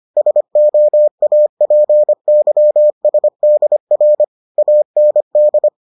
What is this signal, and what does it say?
Morse code, heard as a keyed tone, about 25 wpm, 600 Hz. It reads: SOAPYSDR AND